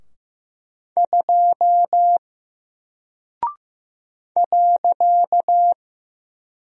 Morse code, heard as a keyed tone, about 15 wpm, 700 Hz. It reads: 2 .
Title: Single Character (Letter:Number:Punctuation:Prosign) 15wpm